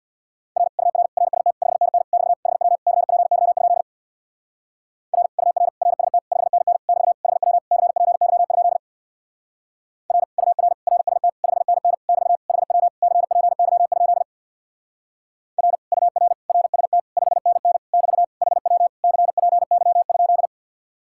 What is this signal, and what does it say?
Morse code, heard as a keyed tone, about 65 wpm, 700 Hz. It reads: R UR RST 5NN <BT> HW CPY? R UR RST 5NN <BT> HW CPY? R UR RST 5NN <BT> HW CPY? R UR RST 5NN <BT> HW CPY?